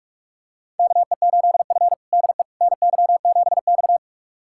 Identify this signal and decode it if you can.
Morse code, heard as a keyed tone, about 46 wpm, 700 Hz. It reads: KE8F DE NY7X
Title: USA Callsign DE Callsign - 4 Characters - 45wpm